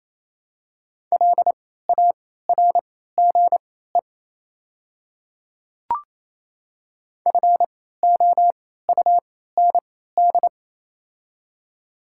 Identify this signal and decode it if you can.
Morse code, heard as a keyed tone, about 28 wpm, spaced as for 16 wpm, 700 Hz. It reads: LARGE FOUND
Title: 3x Character Spacing - Top 500 Words - 28wpm